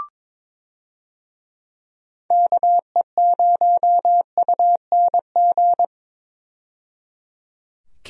Morse code, heard as a keyed tone, about 22 wpm, 700 Hz. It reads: KE0UNG
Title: USA Call Signs - 6 Characters - 22wpm